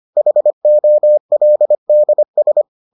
Morse code, heard as a keyed tone, about 25 wpm, 600 Hz. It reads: HOLDS